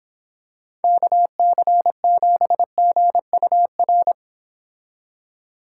Morse code, heard as a keyed tone, about 26 wpm, 700 Hz. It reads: KC7GUR